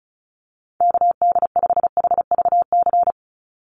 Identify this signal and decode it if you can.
Morse code, heard as a keyed tone, about 34 wpm, 700 Hz. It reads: KD5HVC
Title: USA Call Signs 35wpm